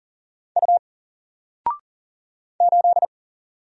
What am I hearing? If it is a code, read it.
Morse code, heard as a keyed tone, about 40 wpm, 700 Hz. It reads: U 8
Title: Single Letter-Number Alt 40wpm